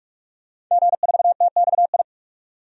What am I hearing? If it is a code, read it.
Morse code, heard as a keyed tone, about 41 wpm, 700 Hz. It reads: G4TXI
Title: DXCC-100 Call Signs - 5 Characters - 45wpm